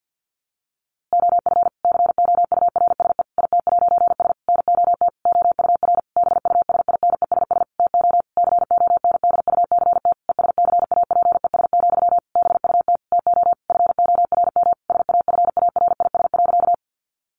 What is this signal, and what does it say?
Morse code, heard as a keyed tone, about 50 wpm, 700 Hz. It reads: OF COURSE IT'S NOT OUR BUSINESS TO CONDUCT ESCAPES, BUT TO FORM SAFARIS.
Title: A Sound Of Thunder 50wpm